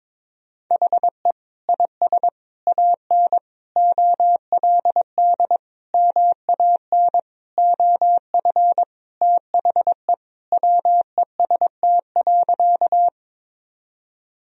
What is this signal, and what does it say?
Morse code, heard as a keyed tone, about 22 wpm, 700 Hz. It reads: HE IS AN OLD MAN OF THE WEST.
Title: Sentences from Top 1000 Words - Review Part 2 - 22wpm